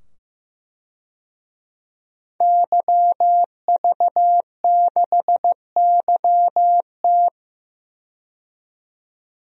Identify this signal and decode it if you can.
Morse code, heard as a keyed tone, about 15 wpm, 700 Hz. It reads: YV6YT
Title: DXCC-100 Call Signs - 5 Characters - 15wpm